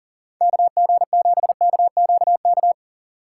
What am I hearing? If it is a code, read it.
Morse code, heard as a keyed tone, about 40 wpm, 700 Hz. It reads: KG7KQK
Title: USA Call Signs - 6 Characters - 2x Character Spacing - 40wpm